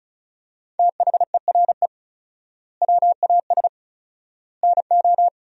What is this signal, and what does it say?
Morse code, heard as a keyed tone, about 34 wpm, 700 Hz. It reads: THERE WAS NO
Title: 4x Word-Spacing - Sentences from Top 300 Words - 35wpm